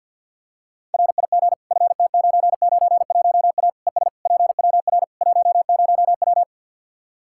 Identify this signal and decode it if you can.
Morse code, heard as a keyed tone, about 51 wpm, 700 Hz. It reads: RIG FT991A ES PWR 10W
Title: Practice QSO - Exchange 2 - Info - 50wpm